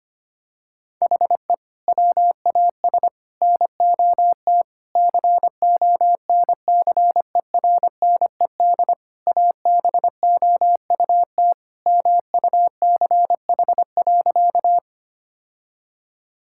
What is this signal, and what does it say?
Morse code, heard as a keyed tone, about 25 wpm, 700 Hz. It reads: HE WAS NOT CONCERNED ABOUT MUCH.
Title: Sentences from Top 1000 Words - Review Part 6 - 25wpm